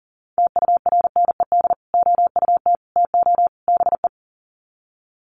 Morse code, heard as a keyed tone, about 40 wpm, 700 Hz. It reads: TURNED OUT TO BE
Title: Sets of 4 Words 40wpm